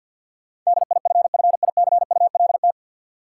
Morse code, heard as a keyed tone, about 53 wpm, 700 Hz. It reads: DIFFICULT
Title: Sentences from Top 600 Words - New Word Drill - 50wpm